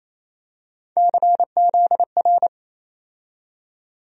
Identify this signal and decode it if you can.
Morse code, heard as a keyed tone, about 28 wpm, 700 Hz. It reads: CZR
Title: USA Call Sign Postfixes - 3 Letters 28wpm